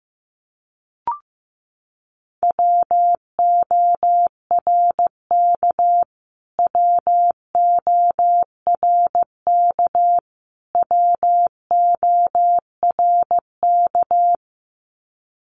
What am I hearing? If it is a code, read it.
Morse code, heard as a keyed tone, about 15 wpm, 700 Hz. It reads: WORK WORK WORK
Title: Top 300 Words Repeated 15wpm